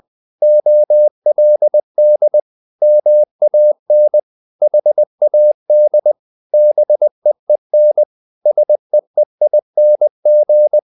Morse code, heard as a keyed tone, about 20 wpm, 600 Hz. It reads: OLD MAN HAD BEEN SEEING
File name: the old man and the sea part 2 20wpm